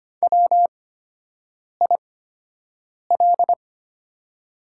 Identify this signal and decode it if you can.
Morse code, heard as a keyed tone, about 25 wpm, spaced as for 7 wpm, 700 Hz. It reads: WIL
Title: 8x Character Spacing - Top 100 Words - 25wpm